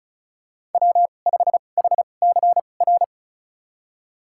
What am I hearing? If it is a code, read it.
Morse code, heard as a keyed tone, about 36 wpm, spaced as for 25 wpm, 700 Hz. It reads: W5HCR
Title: USA Call Signs - 2x Character Spacing - 35wpm